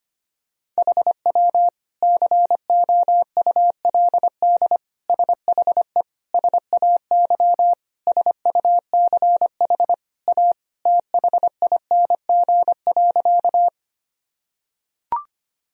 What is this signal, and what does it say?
Morse code, heard as a keyed tone, about 25 wpm, 700 Hz. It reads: HW COULD SHE SAY SUCH A THING.
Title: Sentences from Top 200 Words and Abbreviations 25wpm